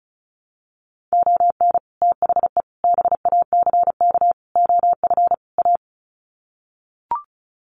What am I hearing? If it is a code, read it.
Morse code, heard as a keyed tone, about 34 wpm, 700 Hz. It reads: ON THE BACK OF A
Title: Sets of 5 Words Encore2 35wpm